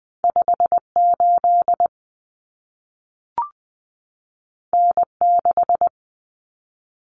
Morse code, heard as a keyed tone, about 20 wpm, 700 Hz. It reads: 58 N6
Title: Common Cut Numbers - 2-Digit 20wpm